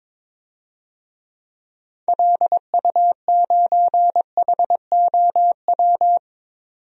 Morse code, heard as a keyed tone, about 22 wpm, 700 Hz. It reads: LU9HOW
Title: DXCC-100 Call Signs - 6 Characters - 22wpm